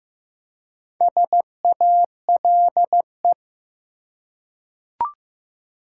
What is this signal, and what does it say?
Morse code, heard as a keyed tone, about 15 wpm, 700 Hz. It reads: SALE
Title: Four Letter Words - 15wpm